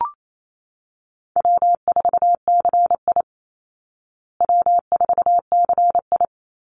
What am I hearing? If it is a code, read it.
Morse code, heard as a keyed tone, about 28 wpm, 700 Hz. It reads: W4CI W4CI
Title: Easy Call Signs Repeated 2x - 28wpm